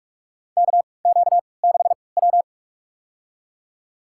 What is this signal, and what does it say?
Morse code, heard as a keyed tone, about 45 wpm, spaced as for 25 wpm, 700 Hz. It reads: KQ6W
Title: USA Call Signs - 4 Characters - 3x Character Spacing - 45wpm